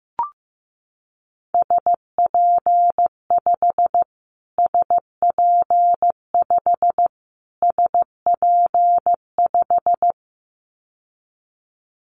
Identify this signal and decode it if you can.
Morse code, heard as a keyed tone, about 15 wpm, 700 Hz. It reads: SP5 SP5 SP5
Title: Callsign Prefixes for Top 25 DXCC - Repeated 3x - 15wpm